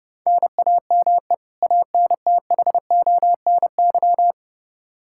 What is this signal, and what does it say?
Morse code, heard as a keyed tone, about 30 wpm, 700 Hz. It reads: NAME ANTHONY
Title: K1USN SST QSO Practice 30wpm